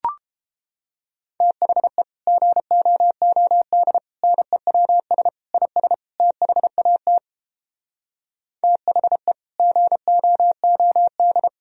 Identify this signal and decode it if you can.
Morse code, heard as a keyed tone, about 32 wpm, 700 Hz. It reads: THE GOOD NEWS IS THAT THE GOOD